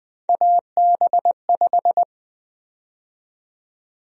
Morse code, heard as a keyed tone, about 20 wpm, 700 Hz. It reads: AB5
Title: USA Call Sign Prefixes - 20wpm